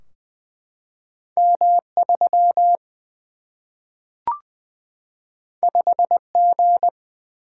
Morse code, heard as a keyed tone, about 20 wpm, 700 Hz. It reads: M3 5G
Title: Single-Letter Single-Number 20wpm